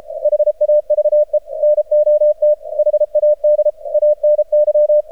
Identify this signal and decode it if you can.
Morse code, heard as a keyed tone, about 33 wpm, 600 Hz. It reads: HAVE NOT HAD ANY
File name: archived / CRSnet